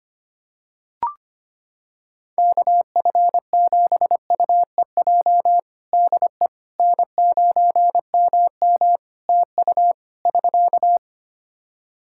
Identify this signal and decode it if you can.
Morse code, heard as a keyed tone, about 25 wpm, 700 Hz. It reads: KF7UEJ DE N9MM TU <SK>
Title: Practice QSO - Exchange 4 - Ending - 25wpm